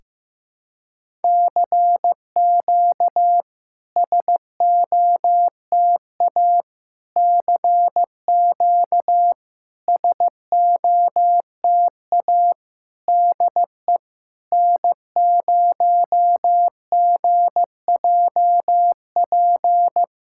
Morse code, heard as a keyed tone, about 14 wpm, 700 Hz. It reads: CQ SOTA CQ SOTA DE N0GJP
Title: SOTA QSO Practice 15wpm